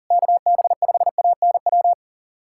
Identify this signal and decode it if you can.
Morse code, heard as a keyed tone, about 40 wpm, 700 Hz. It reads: KB5ANW
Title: USA Callsign DE Callsign - 6 Characters - 40wpm